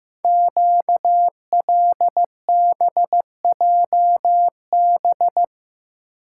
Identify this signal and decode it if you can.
Morse code, heard as a keyed tone, about 15 wpm, 700 Hz. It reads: QLBJB
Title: Code Groups - 5 Letters - 15wpm